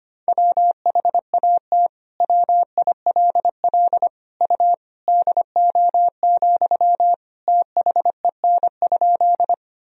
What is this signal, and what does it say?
Morse code, heard as a keyed tone, about 25 wpm, 700 Hz. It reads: WHAT WILL U DO, THEN?
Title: Sentences from Top 100 Words and Abbreviations 25wpm